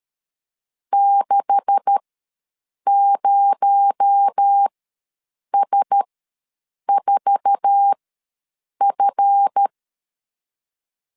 Morse code, heard as a keyed tone, about 13 wpm, spaced as for 7 wpm, 800 Hz. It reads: 60S4F
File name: LEZIONI / 9 286 ws 12wpm 3351